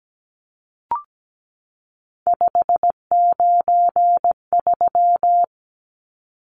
Morse code, heard as a keyed tone, about 17 wpm, 700 Hz. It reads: 593
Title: Numbers - 3-Digit 17wpm